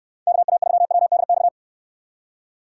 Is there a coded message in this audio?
Yes: Morse code from 0.3 to 1.5 s, heard as a keyed tone, about 65 wpm, 700 Hz. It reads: KA3WDX